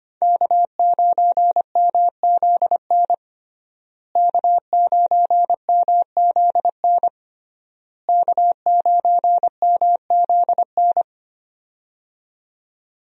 Morse code, heard as a keyed tone, about 25 wpm, 700 Hz. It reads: K9MZN K9MZN K9MZN
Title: USA Call Signs - 5 Characters - Repeated 3x - 25wpm